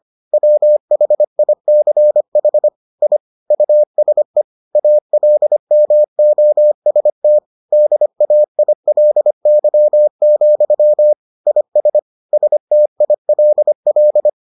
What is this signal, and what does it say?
Morse code, heard as a keyed tone, about 25 wpm, 600 Hz. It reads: WHICH I USE ALMOST DAILY, IS STILL